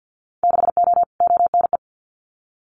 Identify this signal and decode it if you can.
Morse code, heard as a keyed tone, about 50 wpm, 700 Hz. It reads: 6O ONE
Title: Winter Field Day Exchanges Encore 50wpm